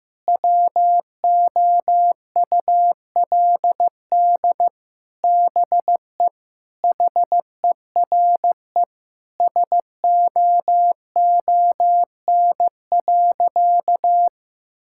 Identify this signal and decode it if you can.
Morse code, heard as a keyed tone, about 15 wpm, 700 Hz. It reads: WOULD BE HERE SOON.
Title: Sentences from Top 1600 Words 15wpm